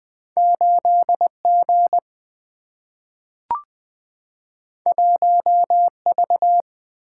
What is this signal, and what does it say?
Morse code, heard as a keyed tone, about 20 wpm, 700 Hz. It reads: 8 G 1 V
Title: Single-Letter Single-Number 20wpm